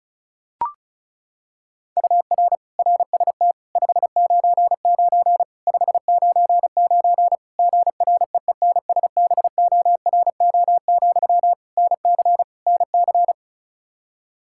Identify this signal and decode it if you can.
Morse code, heard as a keyed tone, about 34 wpm, 700 Hz. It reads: UR RST 599 599 GREENSBORO, NC NC
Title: K1USN SST QSO Practice 35wpm